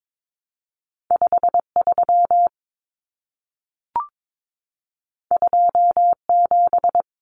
Morse code, heard as a keyed tone, about 22 wpm, 700 Hz. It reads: 53 27
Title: Common Cut Numbers - 2-Digit 22wpm